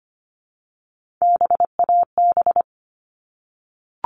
Morse code, heard as a keyed tone, about 25 wpm, 700 Hz. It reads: BAB
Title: All Cut Numbers - 3-Digit 25wpm